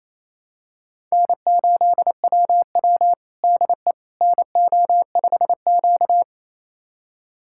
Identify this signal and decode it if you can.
Morse code, heard as a keyed tone, about 28 wpm, 700 Hz. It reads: N8WW DE NO5Q
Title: USA Callsign DE Callsign - 4 Characters - 28wpm